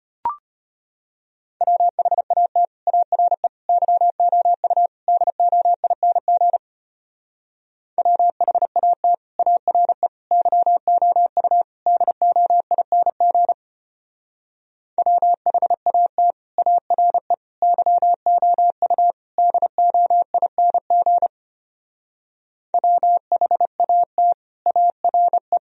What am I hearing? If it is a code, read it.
Morse code, heard as a keyed tone, about 31 wpm, 700 Hz. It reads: WHAT ARE YOU DOING WHAT ARE YOU DOING WHAT ARE YOU DOING WHAT ARE
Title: Speed-Racing - Sets of 4 Words 25wpm